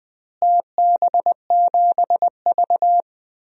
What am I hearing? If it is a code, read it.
Morse code, heard as a keyed tone, about 19 wpm, 700 Hz. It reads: TB7V
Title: DXCC-100 Call Signs - 4 Characters - Repeated 3x - 20wpm